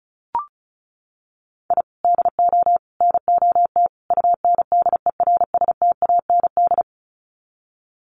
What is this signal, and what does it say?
Morse code, heard as a keyed tone, about 34 wpm, 700 Hz. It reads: I DO NOT UNDERSTAND
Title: Sets of 4 Words 35wpm